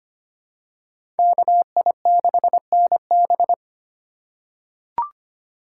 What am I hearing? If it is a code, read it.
Morse code, heard as a keyed tone, about 25 wpm, 700 Hz. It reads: KI6NB
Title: USA Call Signs - 5 Characters - 4x Character Spacing - 25wpm